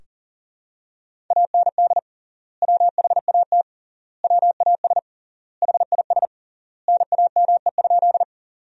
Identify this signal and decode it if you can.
Morse code, heard as a keyed tone, about 40 wpm, 700 Hz. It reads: AND WHAT WAS HIS NAME?
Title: 3x Word-Spacing - Sentences from Top 200 Words - 40wpm